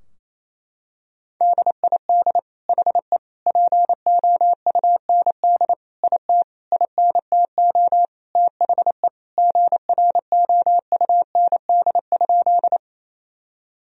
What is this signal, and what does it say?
Morse code, heard as a keyed tone, about 28 wpm, 690 Hz. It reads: DID HE POUND IT INTO THE GROUND?
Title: Sentences from Top 1000 Words - Review Part 2 - 28wpm